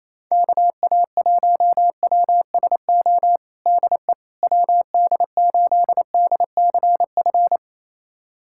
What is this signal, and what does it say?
Morse code, heard as a keyed tone, about 28 wpm, 700 Hz. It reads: KA1WSO DE WD8DCF
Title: USA Callsign DE Callsign - 6 Characters - 28wpm